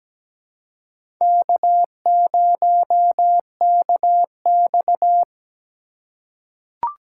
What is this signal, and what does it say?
Morse code, heard as a keyed tone, about 17 wpm, 700 Hz. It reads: K0KX
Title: K1USN SST QSO Practice - 17wpm